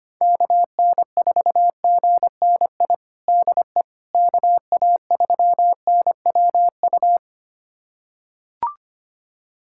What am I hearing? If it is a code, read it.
Morse code, heard as a keyed tone, about 25 wpm, 700 Hz. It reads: KN4GNI DE KA3NWU
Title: USA Callsign DE Callsign - 6 Characters - 25wpm